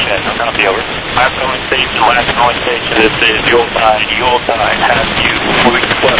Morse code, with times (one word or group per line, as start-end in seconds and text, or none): none